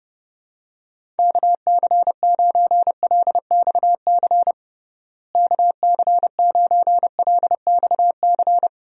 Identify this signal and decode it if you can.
Morse code, heard as a keyed tone, about 30 wpm, 700 Hz. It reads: KC9LXC KC9LXC
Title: USA Call Signs - 6 Characters - Repeated 2x - 30wpm